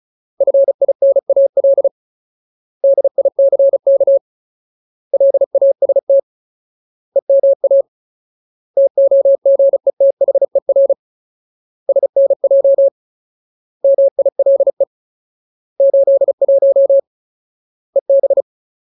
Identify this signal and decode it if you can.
Morse code, heard as a keyed tone, about 34 wpm, 550 Hz. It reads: FINAL DICK LAST EMA TOGETHER SNJ MILE 81 EB
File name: Ultimate Word List / 35WPM Ultimate